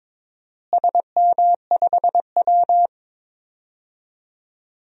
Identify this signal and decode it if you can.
Morse code, heard as a keyed tone, about 21 wpm, 700 Hz. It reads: SM5W
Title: DXCC-100 Call Signs - 4 Characters - Repeated 3x - 22wpm